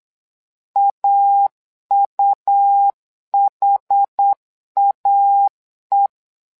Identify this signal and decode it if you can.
Morse code, heard as a keyed tone, about 8 wpm, 800 Hz. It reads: AUHAE
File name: LEZIONI / 2 UHDN / 2 UHD 8wpm 2258